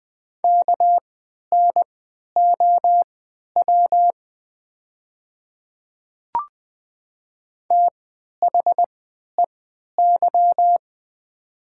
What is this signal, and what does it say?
Morse code, heard as a keyed tone, about 20 wpm, spaced as for 11 wpm, 700 Hz. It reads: KNOW THEY
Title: 3x Character Spacing - Top 100 Words - 20wpm